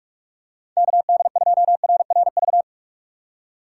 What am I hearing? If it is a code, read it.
Morse code, heard as a keyed tone, about 46 wpm, 700 Hz. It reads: KD2RRV
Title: USA Call Signs - 2x Character Spacing - 45wpm